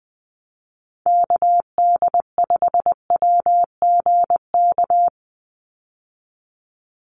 Morse code, heard as a keyed tone, about 20 wpm, 700 Hz. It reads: KD5WGK
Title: USA Call Signs 20wpm